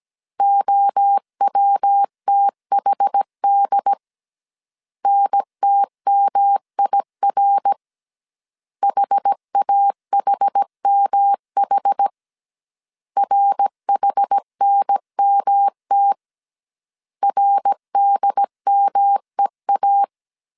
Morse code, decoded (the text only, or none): OWTHD NTMIR HAHMH RHNMT RDMEA